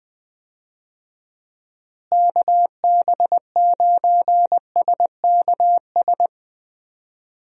Morse code, heard as a keyed tone, about 20 wpm, 700 Hz. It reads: KB9SKS
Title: USA Call Signs - 6 Characters - Repeated 2x - 20wpm